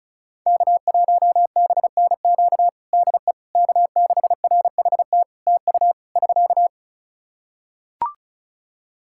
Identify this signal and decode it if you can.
Morse code, heard as a keyed tone, about 36 wpm, 700 Hz. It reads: K1BNQ DE K6RHT TU <SK>
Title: K1USN SST QSO Practice 35wpm